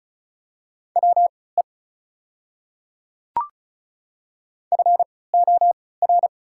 Morse code, heard as a keyed tone, about 36 wpm, spaced as for 20 wpm, 700 Hz. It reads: WE FOR
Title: 3x Character Spacing - Top 200 Words - 35wpm